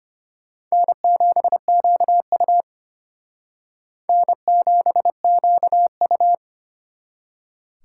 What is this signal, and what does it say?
Morse code, heard as a keyed tone, about 33 wpm, 700 Hz. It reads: N7QU N7QU